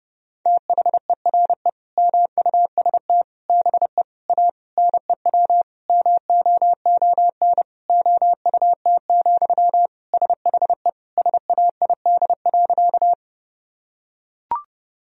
Morse code, heard as a keyed tone, about 32 wpm, 700 Hz. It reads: THERE MUST BE A NEW MOON OUT, SHE SAID.
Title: Sentences from Top 1000 Words - Review Part 3 - 30wpm